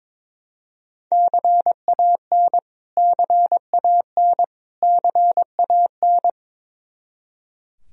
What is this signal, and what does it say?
Morse code, heard as a keyed tone, about 22 wpm, 700 Hz. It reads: CAN CAN CAN
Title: Top 100 Words Repeated 22wpm